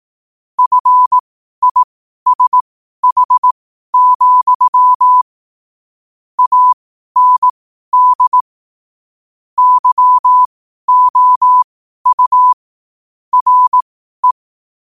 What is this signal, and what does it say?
Morse code, heard as a keyed tone, about 18 wpm, spaced as for 12 wpm, 1000 Hz. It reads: FISH, AND YOU RE
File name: qotdfiles / 12wpm